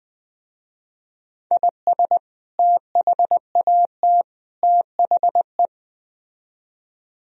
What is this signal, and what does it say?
Morse code, heard as a keyed tone, about 20 wpm, 700 Hz. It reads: IS THAT THE